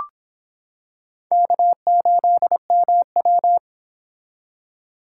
Morse code, heard as a keyed tone, about 26 wpm, 700 Hz. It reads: K8MW